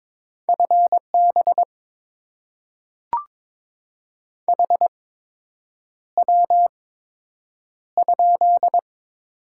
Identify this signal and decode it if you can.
Morse code, heard as a keyed tone, about 22 wpm, spaced as for 8 wpm, 700 Hz. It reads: FB HW?
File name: Top 20 QSO Elements - 8x Character Spacing - 22wpm